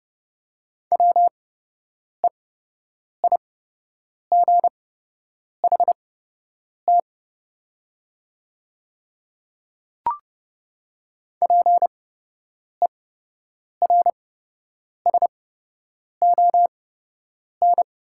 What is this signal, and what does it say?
Morse code, heard as a keyed tone, about 31 wpm, spaced as for 8 wpm, 700 Hz. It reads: WEIGHT PERSON
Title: 8x Character Spacing - Top 500 Words - 30wpm